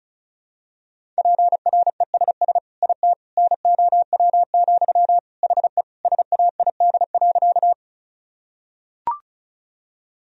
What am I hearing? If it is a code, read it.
Morse code, heard as a keyed tone, about 34 wpm, 700 Hz. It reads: PRESS IT NOW, HE SAID.
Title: Sentences from Top 1000 Words - Review Part 1 - 35wpm